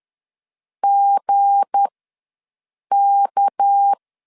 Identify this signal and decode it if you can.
Morse code, heard as a keyed tone, about 10 wpm, spaced as for 6 wpm, 800 Hz. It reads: GK
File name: LEZIONI / 7 GF9Z / 7 G ws 10wpm 3843